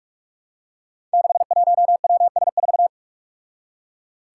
Morse code, heard as a keyed tone, about 46 wpm, 700 Hz. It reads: 61WS4
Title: Five Challenging Letter-Number Combinations 45wpm